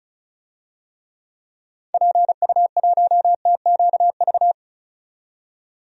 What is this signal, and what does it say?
Morse code, heard as a keyed tone, about 36 wpm, 700 Hz. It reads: PU1TQV
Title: DXCC-100 Call Signs - 6 Characters - 35wpm